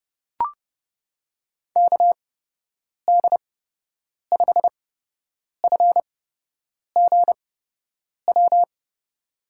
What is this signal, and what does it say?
Morse code, heard as a keyed tone, about 30 wpm, spaced as for 8 wpm, 700 Hz. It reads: KD5FGW
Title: USA Call Signs - 6 Characters - 8x Character Spacing - 30wpm